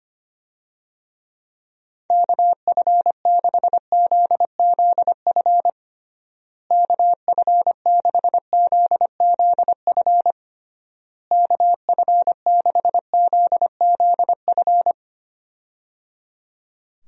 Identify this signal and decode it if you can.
Morse code, heard as a keyed tone, about 25 wpm, 700 Hz. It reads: KF6ZZF KF6ZZF KF6ZZF
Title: USA Call Signs - 6 Characters - Repeated 3x - 25wpm